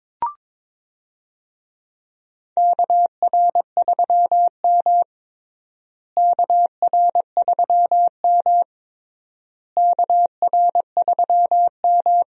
Morse code, heard as a keyed tone, about 22 wpm, 700 Hz. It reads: KR3M KR3M KR3M
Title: USA Call Signs - 4 Characters - Repeated 3x - 22wpm